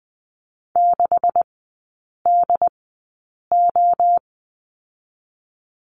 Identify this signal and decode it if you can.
Morse code, heard as a keyed tone, about 19 wpm, spaced as for 8 wpm, 700 Hz. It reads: 6DO
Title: 218 Lesson 35e - Number 0 - Three Characters 20wpm